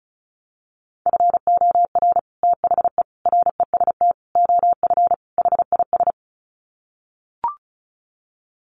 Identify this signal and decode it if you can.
Morse code, heard as a keyed tone, about 34 wpm, 700 Hz. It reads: FOR THE REST OF HIS
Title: Sets of 5 Words 35wpm